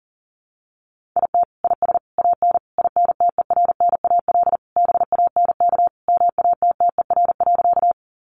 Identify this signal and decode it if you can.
Morse code, heard as a keyed tone, about 40 wpm, 700 Hz. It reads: IT IS AN INTERNAL BANK MATTER.
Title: Sentences from Top 1400 Words 40wpm